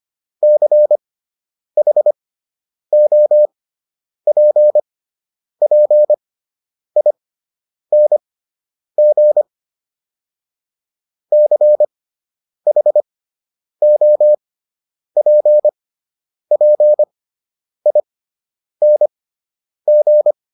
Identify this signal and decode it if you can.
Morse code, heard as a keyed tone, about 25 wpm, spaced as for 9 wpm, 600 Hz. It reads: CHOPPING CHOPPING